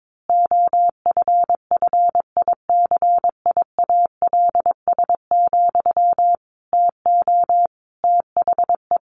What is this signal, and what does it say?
Morse code, heard as a keyed tone, about 22 wpm, 700 Hz. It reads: OFFICIALS, TO THE
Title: A Sound Of Thunder 22wpm